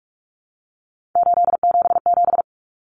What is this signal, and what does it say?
Morse code, heard as a keyed tone, about 44 wpm, 700 Hz. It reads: 877
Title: Common Cut Numbers - 3-Digit 45wpm